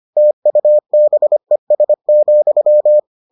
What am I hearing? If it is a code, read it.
Morse code, heard as a keyed tone, about 25 wpm, 600 Hz. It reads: TUBES,